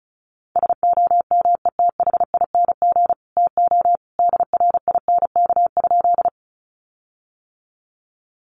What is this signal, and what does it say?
Morse code, heard as a keyed tone, about 34 wpm, 700 Hz. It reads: SOMETHING TO DRINK?
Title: Sentences from Top 1500 Words 35wpm